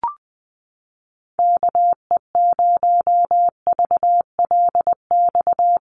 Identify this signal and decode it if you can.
Morse code, heard as a keyed tone, about 20 wpm, 700 Hz. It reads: KE0VLX
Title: USA Call Signs Repeated 3x 20wpm